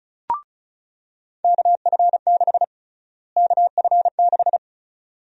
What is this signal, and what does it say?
Morse code, heard as a keyed tone, about 34 wpm, 700 Hz. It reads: KF6 KF6
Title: USA Call Sign Prefixes - Repeated 2x - 35wpm